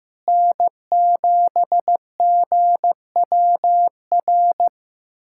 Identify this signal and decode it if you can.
Morse code, heard as a keyed tone, about 15 wpm, 700 Hz. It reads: N7GWR
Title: USA Callsign DE Callsign - 5 Characters - 15wpm